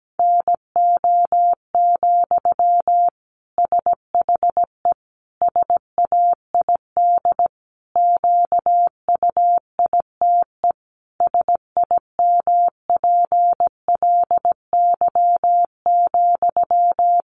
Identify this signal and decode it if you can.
Morse code, heard as a keyed tone, about 17 wpm, 700 Hz. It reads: NO, SHE SAID QUITE SIMPLY,
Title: Sentences from Top 800 Words 17wpm